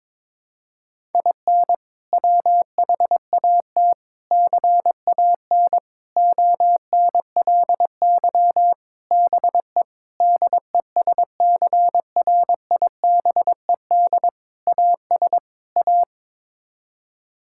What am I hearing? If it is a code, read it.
Morse code, heard as a keyed tone, about 22 wpm, 700 Hz. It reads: IN WHAT CAN ONLY BE DESCRIBED AS A
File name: practice / 22wpm / Sets of 8 Words - 22wpm